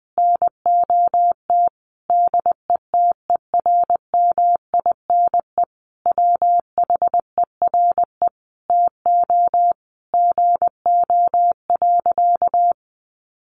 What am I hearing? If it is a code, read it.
Morse code, heard as a keyed tone, about 20 wpm, 700 Hz. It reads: NOT DETERMINE WHERE TO GO.
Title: Sentences from Top 700 Words 20wpm